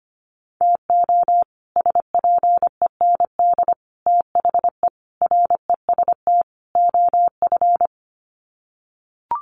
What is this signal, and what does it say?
Morse code, heard as a keyed tone, about 25 wpm, 700 Hz. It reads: TO SPEND THE REST OF
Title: Sets of 5 Words Encore2 25wpm